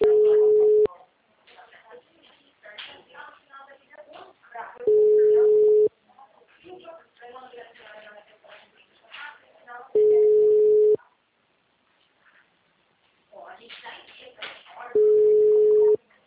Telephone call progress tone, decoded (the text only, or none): ringback tone